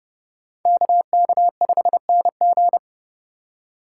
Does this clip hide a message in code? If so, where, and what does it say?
Yes: Morse code from 0.7 to 2.8 s, heard as a keyed tone, about 29 wpm, 700 Hz. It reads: KK5NG